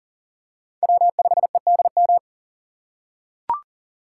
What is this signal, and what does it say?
Morse code, heard as a keyed tone, about 40 wpm, 700 Hz. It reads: W5EDM
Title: USA Call Signs - 5 Characters - 4x Character Spacing - 40wpm